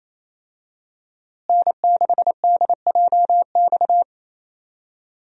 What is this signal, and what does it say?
Morse code, heard as a keyed tone, about 28 wpm, 700 Hz. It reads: N6DJX
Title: USA Call Signs - 5 Characters - Repeated 2x - 28wpm